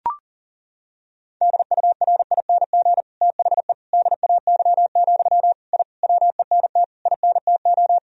Morse code, heard as a keyed tone, about 40 wpm, 700 Hz. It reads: DURING THE DAY, I WENT INTO
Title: Sentences from Top 1000 Words - Review Part 2 - 40wpm